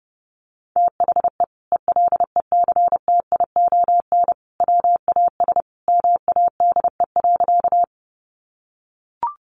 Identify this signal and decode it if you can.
Morse code, heard as a keyed tone, about 30 wpm, 700 Hz. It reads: THE ELECTION WAS MADE.
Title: Sentences from Top 1500 Words 30wpm